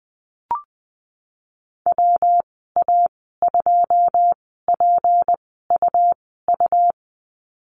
Morse code, heard as a keyed tone, about 20 wpm, spaced as for 14 wpm, 700 Hz. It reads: WA2PUU
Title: 161 Lesson 26g - Letter g - Call Signs 20wpm